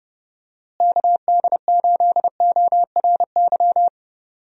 Morse code, heard as a keyed tone, about 31 wpm, 700 Hz. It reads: KD8ORY
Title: USA Call Signs - 6 Characters - Repeated 2x - 30wpm